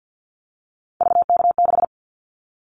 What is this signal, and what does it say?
Morse code, heard as a keyed tone, about 50 wpm, 700 Hz. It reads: VK6
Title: Callsign Prefixes for Top 100 DXCC - 50wpm